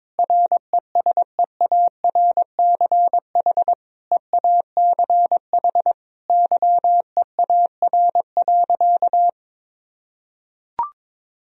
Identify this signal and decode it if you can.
Morse code, heard as a keyed tone, about 22 wpm, 700 Hz. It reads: RESEARCH EACH YEAR.
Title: Sentences from Top 1000 Words - Review Part 4 - 22wpm